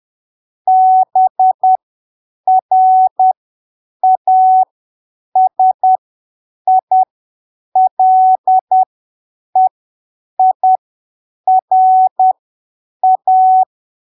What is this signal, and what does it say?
Morse code, heard as a keyed tone, about 10 wpm, spaced as for 7 wpm, 750 Hz. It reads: BRASILEIRA
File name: arquivos / texto / texto 14 10ppm